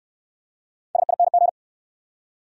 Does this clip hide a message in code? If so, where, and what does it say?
Yes: Morse code from 1.0 to 1.5 s, heard as a keyed tone, about 66 wpm, 700 Hz. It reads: SING